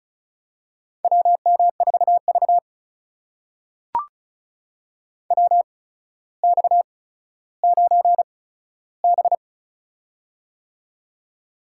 Morse code, heard as a keyed tone, about 36 wpm, spaced as for 15 wpm, 700 Hz. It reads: WM4V WX9B
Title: USA Call Signs - 4 Characters - 8x Character Spacing - 35wpm